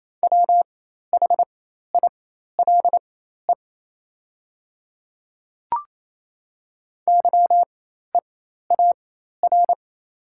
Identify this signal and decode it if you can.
Morse code, heard as a keyed tone, about 28 wpm, spaced as for 13 wpm, 700 Hz. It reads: WHILE YEAR
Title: 4x Character Spacing - Top 500 Words - 28wpm